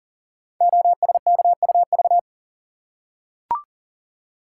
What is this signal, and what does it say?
Morse code, heard as a keyed tone, about 39 wpm, 690 Hz. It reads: OSKUV
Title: Code Groups - 5 Letters - 40wpm